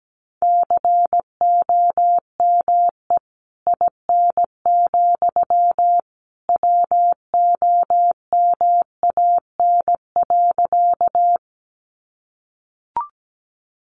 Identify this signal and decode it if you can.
Morse code, heard as a keyed tone, about 17 wpm, 700 Hz. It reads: COME IN, WOMAN.
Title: Sentences from Top 1100 Words 17wpm